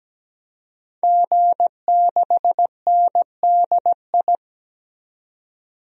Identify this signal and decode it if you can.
Morse code, heard as a keyed tone, about 17 wpm, 700 Hz. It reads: G6NDI
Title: DXCC-100 Call Signs - 5 Characters - 17wpm